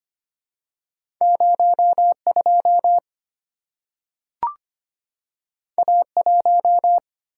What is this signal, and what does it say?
Morse code, heard as a keyed tone, about 25 wpm, 700 Hz. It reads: 02 A1
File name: Two Letter-Number Combinations - 25wpm